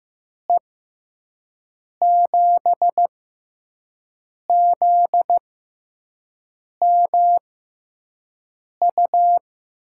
Morse code, heard as a keyed tone, about 15 wpm, spaced as for 5 wpm, 700 Hz. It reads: E7ZMU
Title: USA Call Signs - 6 Characters - 6x Character Spacing - 15wpm